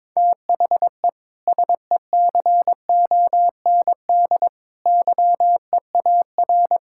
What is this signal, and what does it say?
Morse code, heard as a keyed tone, about 22 wpm, 700 Hz. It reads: THE SECOND YEAR
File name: practice / 22wpm / Sets of 6 Words - 22wpm